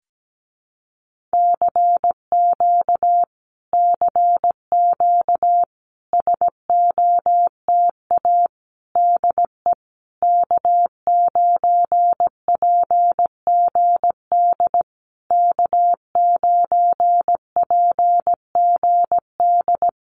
Morse code, heard as a keyed tone, about 17 wpm, 700 Hz. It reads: CQ CQ SOTA DE K9PGD K9PGD
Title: Common QSO Phrases Encore 17wpm